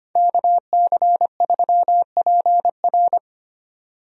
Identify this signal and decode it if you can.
Morse code, heard as a keyed tone, about 25 wpm, 700 Hz. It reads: KC3PR
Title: USA Call Signs - 5 Characters - 25wpm